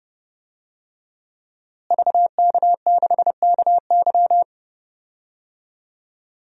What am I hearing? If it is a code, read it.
Morse code, heard as a keyed tone, about 30 wpm, 700 Hz. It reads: VK6KY
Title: DXCC-100 Call Signs - 5 Characters - Repeated 2x - 30wpm